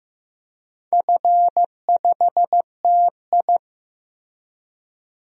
Morse code, heard as a keyed tone, about 16 wpm, 700 Hz. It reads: F5TI
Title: DXCC-100 Call Signs - 4 Characters - Repeated 2x - 15wpm